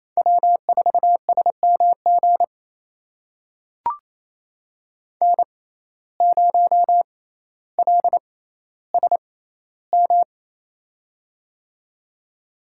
Morse code, heard as a keyed tone, about 28 wpm, spaced as for 14 wpm, 700 Hz. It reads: W4SMG N0LSM
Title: USA Call Signs - 5 Characters - 6x Character Spacing - 28wpm